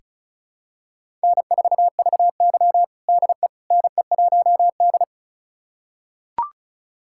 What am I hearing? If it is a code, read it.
Morse code, heard as a keyed tone, about 34 wpm, 700 Hz. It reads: N4VY DE NE1D
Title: USA Callsign DE Callsign - 4 Characters - 35wpm